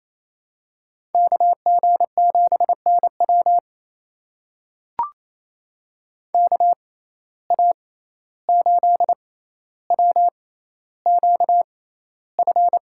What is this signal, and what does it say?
Morse code, heard as a keyed tone, about 28 wpm, spaced as for 14 wpm, 700 Hz. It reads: KG7NW KA8WQF
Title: USA Call Signs - 6x Character Spacing - 28wpm